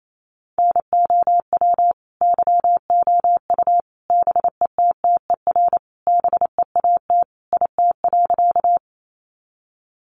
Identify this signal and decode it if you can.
Morse code, heard as a keyed tone, about 28 wpm, 700 Hz. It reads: NOW YOU BETTER BEAT IT.
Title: Sentences from Top 1300 Words 28wpm